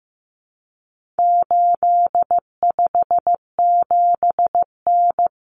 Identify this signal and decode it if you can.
Morse code, heard as a keyed tone, about 15 wpm, 700 Hz. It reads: 857N
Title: Common Cut Numbers - 4-Digit 15wpm